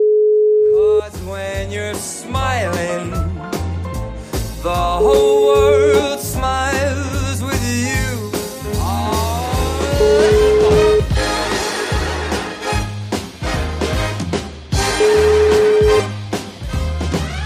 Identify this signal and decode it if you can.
Ringback tone (Europe, 425 Hz), with noise.